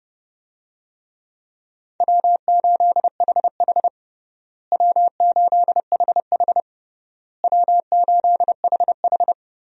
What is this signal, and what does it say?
Morse code, heard as a keyed tone, about 32 wpm, 700 Hz. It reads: W8HH W8HH W8HH
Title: USA Call Signs - 4 Characters - Repeated 3x - 30wpm